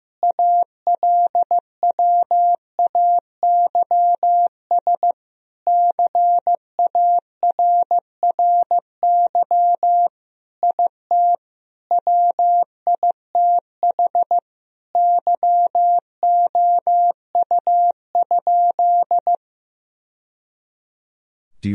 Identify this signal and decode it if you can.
Morse code, heard as a keyed tone, about 15 wpm, 700 Hz. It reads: ALWAYS CARRY IT WITH YOU?
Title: Sentences from Top 1000 Words - Review Part 1 - 15wpm